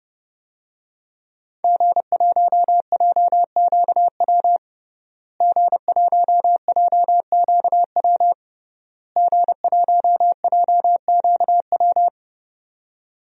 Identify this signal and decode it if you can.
Morse code, heard as a keyed tone, about 32 wpm, 700 Hz. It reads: G1JQW G1JQW G1JQW
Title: DXCC-100 Call Signs - 5 Characters - Repeated 3x - 30wpm